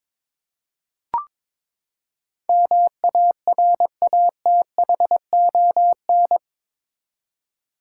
Morse code, heard as a keyed tone, about 22 wpm, 700 Hz. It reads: MARATHON
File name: practice / 22wpm / Eight Letter Words - 22wpm